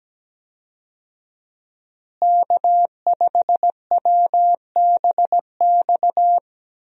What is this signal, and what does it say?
Morse code, heard as a keyed tone, about 17 wpm, 700 Hz. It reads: K5WBX
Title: USA Call Signs - 5 Characters - Repeated 2x - 17wpm